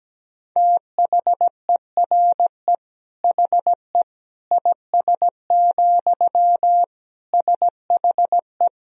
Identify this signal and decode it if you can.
Morse code, heard as a keyed tone, about 17 wpm, 700 Hz. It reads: THERE HE IS, SHE
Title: Sentences from Top 100 Words and Abbreviations 17wpm